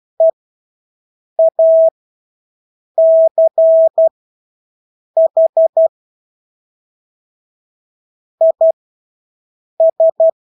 Morse code, heard as a keyed tone, about 12 wpm, spaced as for 6 wpm, 650 Hz. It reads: EACH IS
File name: Engelske ord 12-6 wpm